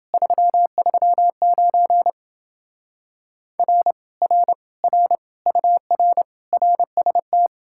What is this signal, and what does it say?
Morse code, heard as a keyed tone, about 28 wpm, 700 Hz. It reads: 339 R R R UR RST